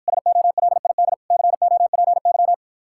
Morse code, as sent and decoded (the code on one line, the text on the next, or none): ... --- .-.. .. -.. / -.-. --- .--. -.--
SOLID COPY